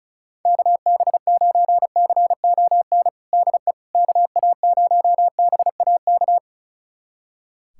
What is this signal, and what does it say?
Morse code, heard as a keyed tone, about 36 wpm, 700 Hz. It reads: KB9CON DE KA0BAK
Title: USA Callsign DE Callsign 35wpm